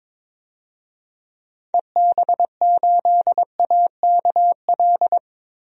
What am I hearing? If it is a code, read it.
Morse code, heard as a keyed tone, about 22 wpm, 700 Hz. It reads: EB8AKL
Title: DXCC-100 Call Signs - 6 Characters - 22wpm